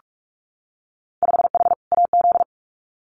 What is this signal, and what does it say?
Morse code, heard as a keyed tone, about 46 wpm, 690 Hz. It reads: 5H AZ
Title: Winter Field Day Exchanges Encore 45wpm